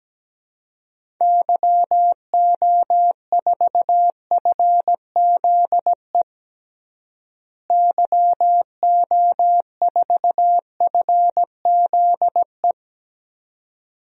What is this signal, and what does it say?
Morse code, heard as a keyed tone, about 17 wpm, 700 Hz. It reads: YO4FZE YO4FZE
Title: DXCC-100 Call Signs - 6 Characters - Repeated 2x - 17wpm